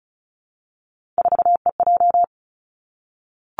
Morse code, heard as a keyed tone, about 36 wpm, 700 Hz. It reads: 4EJ
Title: Three Challenging Letter-Number Combinations 35wpm